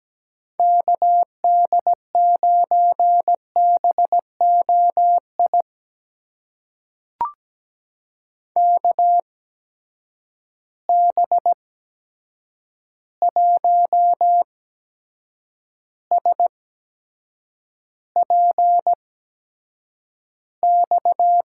Morse code, heard as a keyed tone, about 17 wpm, spaced as for 7 wpm, 700 Hz. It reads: KD9BOI KB1SPX
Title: USA Call Signs - 6 Characters - 8x Character Spacing - 17wpm